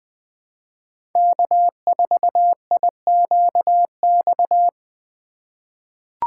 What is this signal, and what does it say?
Morse code, heard as a keyed tone, about 20 wpm, 700 Hz. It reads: K4IQX
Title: USA Call Signs - 6x Character Spacing - 20wpm